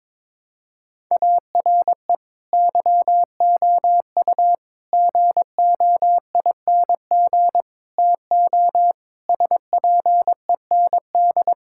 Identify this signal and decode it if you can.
Morse code, heard as a keyed tone, about 22 wpm, 700 Hz. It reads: ARE YOU GOING TO SPEND
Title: Sentences from Top 1000 Words - Review Part 5 - 22wpm